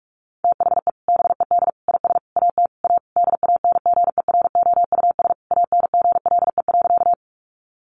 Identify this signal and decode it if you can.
Morse code, heard as a keyed tone, about 44 wpm, 700 Hz. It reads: THE BED IS AT A DANGEROUS ANGLE.
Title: Sentences from Top 1500 Words 45wpm